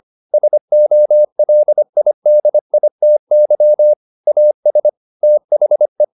Morse code, heard as a keyed tone, about 25 wpm, 600 Hz. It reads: SOLIDITY AS THE